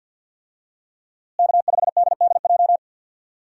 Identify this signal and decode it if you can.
Morse code, heard as a keyed tone, about 52 wpm, 700 Hz. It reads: K5DDJ
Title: USA Call Signs - 5 Characters - Repeated 2x - 50wpm